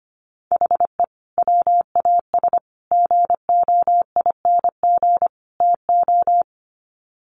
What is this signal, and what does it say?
Morse code, heard as a keyed tone, about 25 wpm, 700 Hz. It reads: HE WAS GOING TO
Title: Sets of 4 Words 25wpm